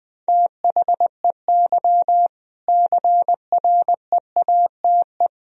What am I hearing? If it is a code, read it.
Morse code, heard as a keyed tone, about 20 wpm, 700 Hz. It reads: THEY CREATE
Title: Sentences from Top 1000 Words - Review Part 3 - 20wpm